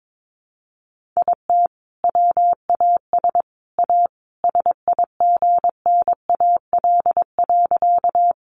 Morse code, heard as a keyed tone, about 22 wpm, 700 Hz. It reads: IT WAS A SIGNAL.
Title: Sentences from Top 1400 Words 22wpm